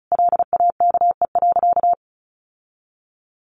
Morse code, heard as a keyed tone, about 34 wpm, 700 Hz. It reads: LAKE.
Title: Sentences from Top 1100 Words 35wpm